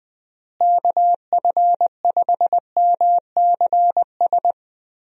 Morse code, heard as a keyed tone, about 20 wpm, 700 Hz. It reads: KF5MCS
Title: USA Call Signs - 6 Characters - Repeated 2x - 20wpm